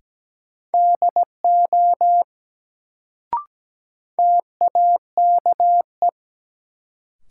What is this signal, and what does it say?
Morse code, heard as a keyed tone, about 17 wpm, 700 Hz. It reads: DO TAKE
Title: Sets of 2 Words - New Word Drill - 17wpm